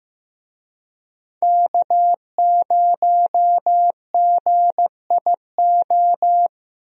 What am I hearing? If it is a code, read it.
Morse code, heard as a keyed tone, about 15 wpm, 700 Hz. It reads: K0GIO
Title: USA Call Signs - 5 Characters - Repeated 3x - 15wpm